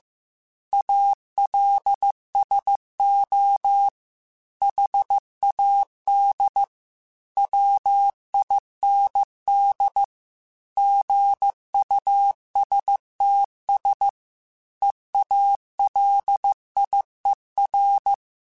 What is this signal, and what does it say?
Morse code, heard as a keyed tone, about 15 wpm, 800 Hz. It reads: ALSO HAD WIND GUSTS EALIER
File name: Weather Sets / Weather Sets-mp3s / Weather 5 5 @13